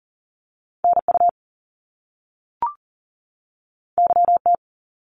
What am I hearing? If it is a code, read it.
Morse code, heard as a keyed tone, about 41 wpm, 700 Hz. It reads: NU YT
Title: Canadian Provinces and Territories 40wpm